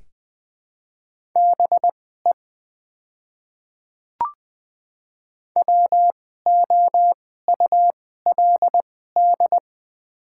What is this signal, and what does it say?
Morse code, heard as a keyed tone, about 20 wpm, spaced as for 14 wpm, 700 Hz. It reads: BE WOULD
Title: 2x Character Spacing - Top 100 Words - 20wpm